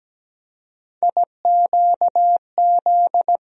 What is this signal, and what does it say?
Morse code, heard as a keyed tone, about 17 wpm, 700 Hz. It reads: IQZ
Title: USA Call Sign Postfixes - 3 Letters 17wpm